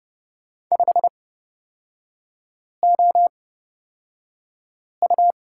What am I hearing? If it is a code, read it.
Morse code, heard as a keyed tone, about 30 wpm, spaced as for 5 wpm, 700 Hz. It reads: 5OU
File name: Lesson 37e - Letter x - Three Characters 30wpm